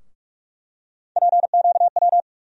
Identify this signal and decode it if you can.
Morse code, heard as a keyed tone, about 46 wpm, 700 Hz. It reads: PQW
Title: USA Call Sign Postfixes - 3 Letters 45wpm